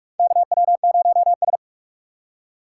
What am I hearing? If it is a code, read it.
Morse code, heard as a keyed tone, about 46 wpm, 700 Hz. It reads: KW0S
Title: USA Call Signs - 4x Character Spacing - 45wpm